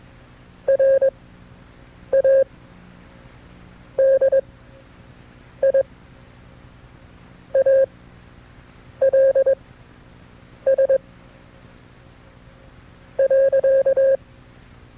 Morse code, heard as a keyed tone, about 21 wpm, spaced as for 6 wpm, 540 Hz. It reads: RADIALS .
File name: FreeFiles / CW-practice / KB6MT-tape2-side2